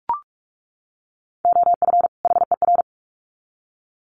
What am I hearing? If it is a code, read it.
Morse code, heard as a keyed tone, about 44 wpm, 700 Hz. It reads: OF HER
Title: Sets of 2 Words 45wpm